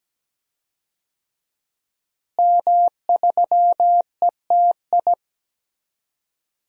Morse code, heard as a keyed tone, about 17 wpm, 700 Hz. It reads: M3ETI
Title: DXCC-100 Call Signs - 5 Characters - Repeated 3x - 17wpm